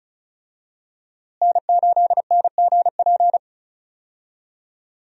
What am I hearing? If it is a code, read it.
Morse code, heard as a keyed tone, about 34 wpm, 700 Hz. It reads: N8NGP
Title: USA Call Signs - 5 Characters - Repeated 2x - 35wpm